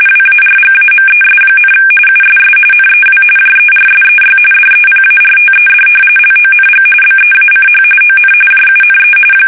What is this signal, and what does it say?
RTTY, about 45.45 baud. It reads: WARGAMES, USS NEW JERSEY DEPARTED LONG BEACH ON JUNE
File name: fsk850 / battleshipnj